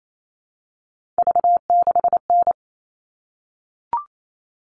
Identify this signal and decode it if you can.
Morse code, heard as a keyed tone, about 28 wpm, 700 Hz. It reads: V6N
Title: All Cut Numbers - 3-Digit 28wpm